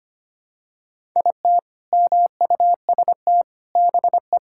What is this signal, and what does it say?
Morse code, heard as a keyed tone, about 25 wpm, 700 Hz. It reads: IT MUST BE
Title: Sentences from Top 1000 Words - Review Part 6 - 25wpm